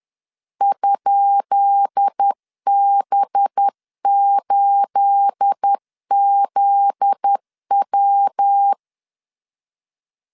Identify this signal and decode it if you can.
Morse code, heard as a keyed tone, about 10 wpm, 800 Hz. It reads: ?B8ZW
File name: LEZIONI / 10 SIGNS / 10B INT 10wpm 2844